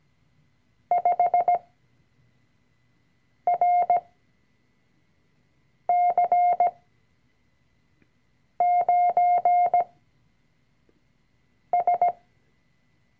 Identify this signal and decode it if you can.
Morse code, heard as a keyed tone, about 17 wpm, spaced as for 4 wpm, 710 Hz. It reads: 5RC9S
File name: CW-Online-Kurs / Anfänger / Lektion 005